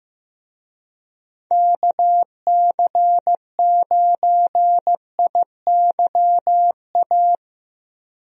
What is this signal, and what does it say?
Morse code, heard as a keyed tone, about 15 wpm, 700 Hz. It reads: KC9IYA
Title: USA Call Signs - 6 Characters - Repeated 3x - 15wpm